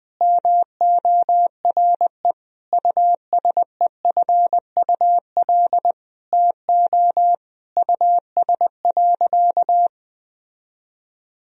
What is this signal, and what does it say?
Morse code, heard as a keyed tone, about 21 wpm, 700 Hz. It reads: MORE USEFUL TO US.
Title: Sentences from Top 1000 Words - Review Part 4 - 20wpm